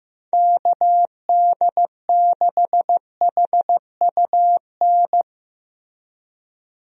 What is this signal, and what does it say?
Morse code, heard as a keyed tone, about 15 wpm, 700 Hz. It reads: KD6HUN
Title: USA Call Signs - 6 Characters - 15wpm